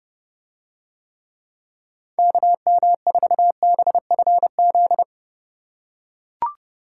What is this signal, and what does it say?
Morse code, heard as a keyed tone, about 31 wpm, 700 Hz. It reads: KM4BFZ